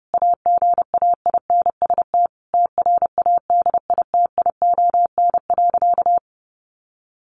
Morse code, heard as a keyed tone, about 28 wpm, 700 Hz. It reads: AGAINST TRADITION.
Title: Sentences from Top 1500 Words 30wpm